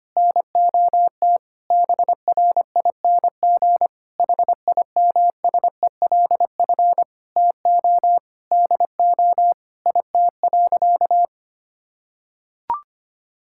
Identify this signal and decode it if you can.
Morse code, heard as a keyed tone, about 25 wpm, 700 Hz. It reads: NOT BRING HIMSELF TO DO IT.
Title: Sentences from Top 1600 Words 25wpm